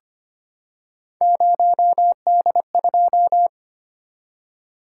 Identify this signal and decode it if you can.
Morse code, heard as a keyed tone, about 25 wpm, 700 Hz. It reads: 0D2
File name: Three Letter-Number Combinations - 25wpm